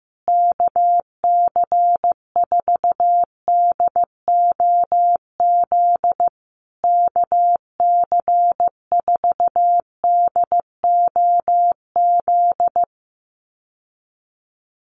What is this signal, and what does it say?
Morse code, heard as a keyed tone, about 14 wpm, 700 Hz. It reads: KC4DOZ KC4DOZ
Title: USA Call Signs Repeated 3x 15wpm